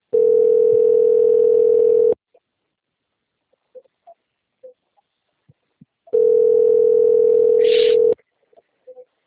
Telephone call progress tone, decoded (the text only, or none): ringback tone